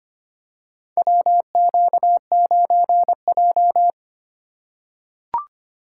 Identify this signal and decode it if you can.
Morse code, heard as a keyed tone, about 25 wpm, 700 Hz. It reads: WQ9J
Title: USA Call Signs - 4 Characters - 2x Character Spacing - 25wpm